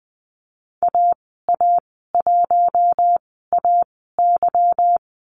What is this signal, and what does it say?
Morse code, heard as a keyed tone, about 19 wpm, spaced as for 14 wpm, 700 Hz. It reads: AA1AY
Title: 154 Lesson 25g - Letter p - Call Signs 20wpm